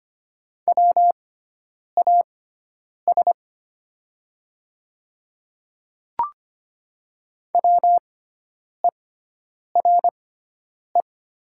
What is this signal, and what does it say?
Morse code, heard as a keyed tone, about 25 wpm, spaced as for 9 wpm, 700 Hz. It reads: WAS WERE
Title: 6x Character Spacing - Top 100 Words - 25wpm